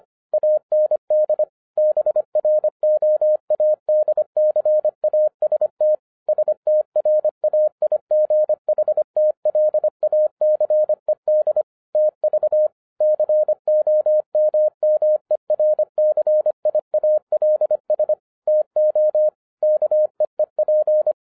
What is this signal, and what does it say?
Morse code, heard as a keyed tone, about 25 wpm, 610 Hz. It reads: AND BROADCAST STRAIGHTLACED TV COMMERCIALS TO KEEP
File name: 2011-07-mp3 / n39 THE 3.11 2 25wpm